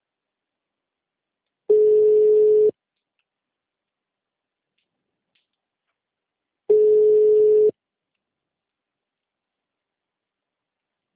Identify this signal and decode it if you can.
Ringback tone (Europe, 425 Hz).
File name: out- -2041- - - .12853